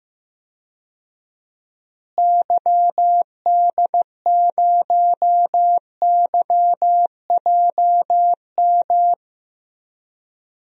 Morse code, heard as a keyed tone, about 15 wpm, 700 Hz. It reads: YD0YJM
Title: DXCC-100 Call Signs - 6 Characters - Repeated 3x - 15wpm